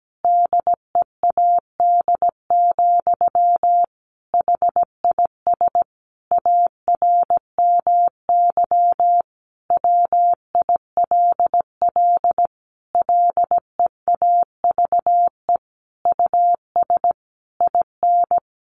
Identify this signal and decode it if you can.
Morse code, heard as a keyed tone, about 17 wpm, 700 Hz. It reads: DEAD, HIS ARMY WILL LEAVE US IN
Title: Sentences from Top 1500 Words 17wpm